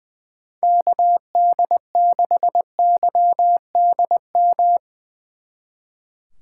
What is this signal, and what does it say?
Morse code, heard as a keyed tone, about 22 wpm, 700 Hz. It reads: KD6YDM
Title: Speed Racing - USA Call Signs 20wpm